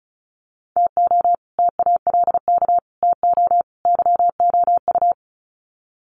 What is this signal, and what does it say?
Morse code, heard as a keyed tone, about 34 wpm, 700 Hz. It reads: TO TALK TO YOU
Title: Sets of 4 Words 35wpm